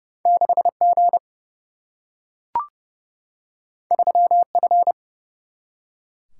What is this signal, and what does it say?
Morse code, heard as a keyed tone, about 30 wpm, 700 Hz. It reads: 6G 3F
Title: Single-Letter Single-Number 30wpm